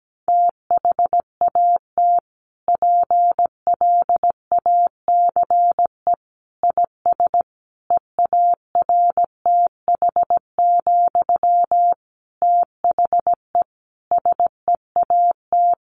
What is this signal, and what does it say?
Morse code, heard as a keyed tone, about 17 wpm, 700 Hz. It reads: THAT PLACE IS EARTH, THE SEAT
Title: Sentences from Top 1500 Words 17wpm